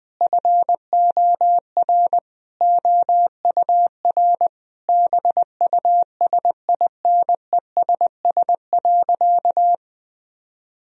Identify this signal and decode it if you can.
Morse code, heard as a keyed tone, about 20 wpm, 700 Hz. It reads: FOR OUR BUSINESS.
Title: Sentences from Top 1000 Words - Review Part 4 - 20wpm